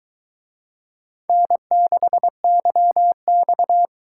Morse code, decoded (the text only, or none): N6YX